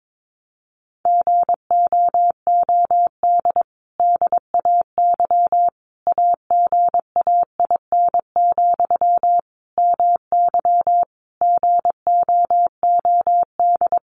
Morse code, heard as a keyed tone, about 22 wpm, 700 Hz. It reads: GOOD DAY AGAIN, MY GOOD
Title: Sentences from Top 1100 Words 22wpm